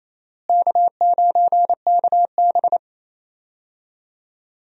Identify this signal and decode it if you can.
Morse code, heard as a keyed tone, about 28 wpm, 700 Hz. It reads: K9KB
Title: USA Call Signs - 4 Characters - Repeated 3x - 28wpm